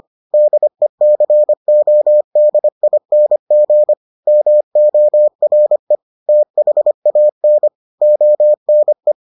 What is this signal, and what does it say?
Morse code, heard as a keyed tone, about 25 wpm, 600 Hz. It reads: DECODING MORE THAN ONE